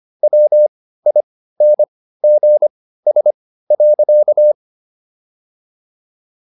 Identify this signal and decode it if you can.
Morse code, heard as a keyed tone, about 25 wpm, spaced as for 15 wpm, 600 Hz. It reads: WINGS.
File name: SS101 15